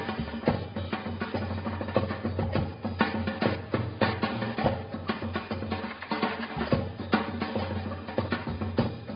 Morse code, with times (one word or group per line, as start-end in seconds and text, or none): none